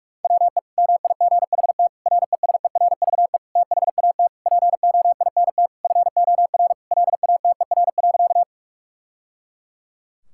Morse code, heard as a keyed tone, about 41 wpm, 700 Hz. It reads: WE MIGHT RESERVE THAT POINT FOR LATER.
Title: Sentences from Top 1000 Words - Review Part 6 - 45wpm